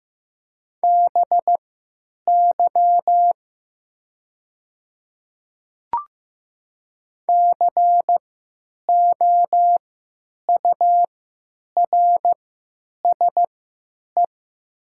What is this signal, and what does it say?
Morse code, heard as a keyed tone, about 15 wpm, spaced as for 8 wpm, 700 Hz. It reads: BY COURSE
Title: 3x Character Spacing - Top 500 Words - 15wpm